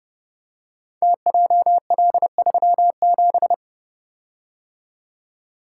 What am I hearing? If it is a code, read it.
Morse code, heard as a keyed tone, about 30 wpm, 700 Hz. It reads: TJL37
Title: Code Groups - 5 Letter:Numbers - 30wpm